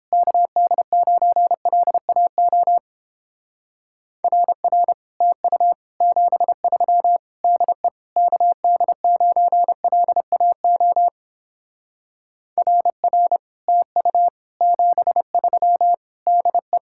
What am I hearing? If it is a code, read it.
Morse code, heard as a keyed tone, about 29 wpm, 700 Hz. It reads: KD9LAO RR TU 73 DE KD9LAO RR TU 73 DE